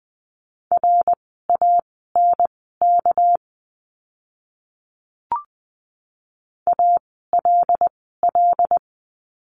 Morse code, heard as a keyed tone, about 20 wpm, spaced as for 15 wpm, 700 Hz. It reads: RANK ALL
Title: 199 Lesson 32f - Letter k - Words 20wpm